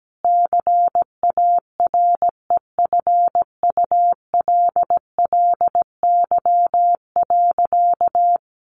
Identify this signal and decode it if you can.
Morse code, heard as a keyed tone, about 17 wpm, 700 Hz. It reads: CAREFULLY.
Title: Sentences from Top 1300 Words 17wpm